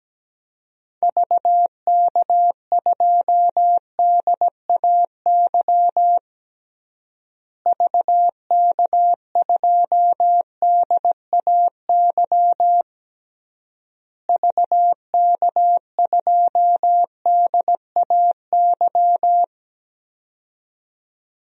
Morse code, heard as a keyed tone, about 17 wpm, 700 Hz. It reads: VK2DAY VK2DAY VK2DAY
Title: DXCC-100 Call Signs - 6 Characters - Repeated 3x - 17wpm